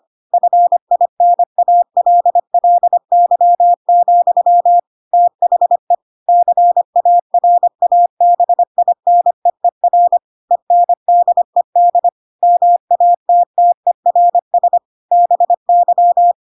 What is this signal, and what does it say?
Morse code, heard as a keyed tone, about 25 wpm, 700 Hz. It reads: FINALLY, THE CARABINEER ENDED MATTERS BY
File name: Pinocchio / Pinocchio ch3 25 25 ik1hge